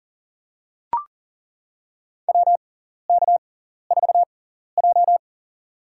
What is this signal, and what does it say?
Morse code, heard as a keyed tone, about 40 wpm, spaced as for 14 wpm, 700 Hz. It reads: WK4J
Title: USA Call Signs - 4 Characters - 6x Character Spacing - 40wpm